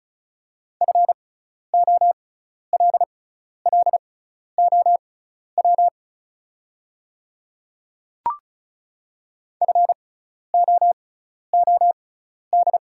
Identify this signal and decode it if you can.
Morse code, heard as a keyed tone, about 36 wpm, spaced as for 12 wpm, 700 Hz. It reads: FOLLOW FOOD
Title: 6x Character Spacing - Top 300 Words - 35wpm